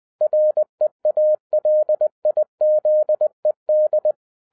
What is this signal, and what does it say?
Morse code, heard as a keyed tone, about 20 wpm, 610 Hz. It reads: REALIZED
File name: 2012-01-mp3 / n16-THROUGH EIGHT MONTHS-20wpm